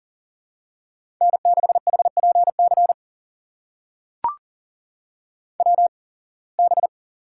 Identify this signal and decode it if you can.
Morse code, heard as a keyed tone, about 40 wpm, spaced as for 26 wpm, 700 Hz. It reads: N6HPC WB
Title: USA Call Signs - 8x Character Spacing - 40wpm